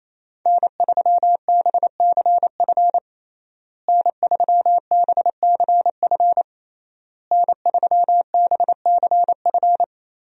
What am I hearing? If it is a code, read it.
Morse code, heard as a keyed tone, about 27 wpm, 700 Hz. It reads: N3BCF N3BCF N3BCF
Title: USA Call Signs - 5 Characters - Repeated 3x - 28wpm